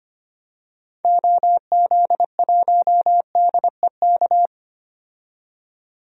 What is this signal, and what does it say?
Morse code, heard as a keyed tone, about 25 wpm, 700 Hz. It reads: OZ1DEK
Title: DXCC-100 Call Signs - 6 Characters - 25wpm